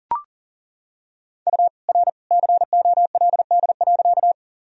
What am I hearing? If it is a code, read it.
Morse code, heard as a keyed tone, about 37 wpm, 700 Hz. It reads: U R COLD.
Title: Sentences from Top 400 Words and Abbreviations 40wpm